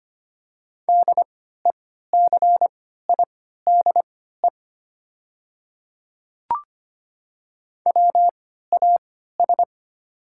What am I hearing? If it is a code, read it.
Morse code, heard as a keyed tone, about 25 wpm, spaced as for 14 wpm, 700 Hz. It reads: DECIDE WAS
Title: 3x Character Spacing - Top 500 Words - 25wpm